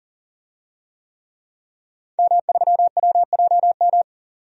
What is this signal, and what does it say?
Morse code, heard as a keyed tone, about 40 wpm, 700 Hz. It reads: M3WJM
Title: DXCC-100 Call Signs - 5 Characters - 40wpm